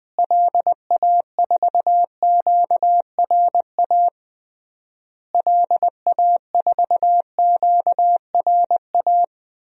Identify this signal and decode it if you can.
Morse code, heard as a keyed tone, about 20 wpm, 700 Hz. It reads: LA4QRA LA4QRA
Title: DXCC-100 Call Signs - 6 Characters - Repeated 3x - 20wpm